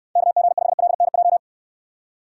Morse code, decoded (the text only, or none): KG5LNJ